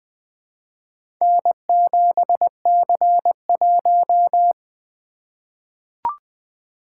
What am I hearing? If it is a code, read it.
Morse code, heard as a keyed tone, about 20 wpm, 700 Hz. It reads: N7C1
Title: Four Letter-Number Combinations - 20wpm